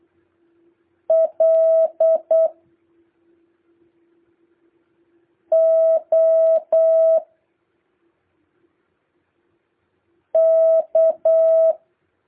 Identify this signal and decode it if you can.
Morse code, heard as a keyed tone, about 8 wpm, 650 Hz. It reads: L O K